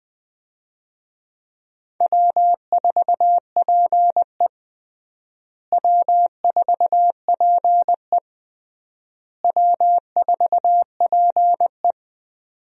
Morse code, heard as a keyed tone, about 20 wpm, 700 Hz. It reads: W4PE W4PE W4PE
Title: USA Call Signs - 4 Characters - Repeated 3x - 20wpm